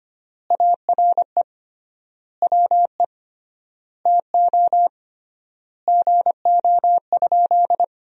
Morse code, heard as a keyed tone, about 25 wpm, 700 Hz. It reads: ARE WE TO GO?
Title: 3x Word-Spacing - Sentences from Top 200 Words - 25wpm